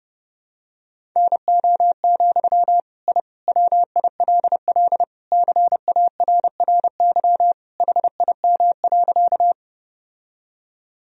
Morse code, heard as a keyed tone, about 30 wpm, 700 Hz. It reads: NO, I WILL CARRY HIM.
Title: Sentences from Top 1000 Words - Review Part 1 - 30wpm